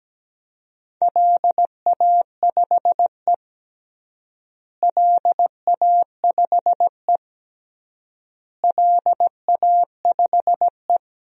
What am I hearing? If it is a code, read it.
Morse code, heard as a keyed tone, about 17 wpm, 700 Hz. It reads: LA5E LA5E LA5E
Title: DXCC-100 Call Signs - 4 Characters - Repeated 3x - 17wpm